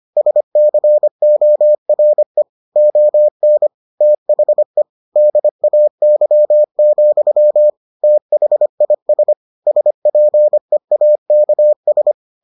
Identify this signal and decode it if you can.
Morse code, heard as a keyed tone, about 25 wpm, 600 Hz. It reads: SCORE ON THE DAY, THIS SPEAKS